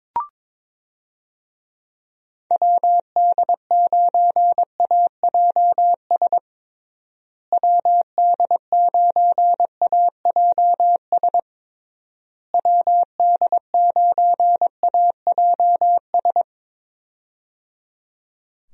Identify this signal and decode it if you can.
Morse code, heard as a keyed tone, about 22 wpm, 700 Hz. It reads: WD9AJS WD9AJS WD9AJS
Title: USA Call Signs - 6 Characters - Repeated 3x - 22wpm